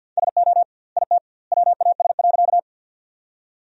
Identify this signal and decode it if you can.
Morse code, heard as a keyed tone, about 50 wpm, 700 Hz. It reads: SO IT WAS.
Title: 2x Word-Spacing - Sentences from Top 100 Words - 50wpm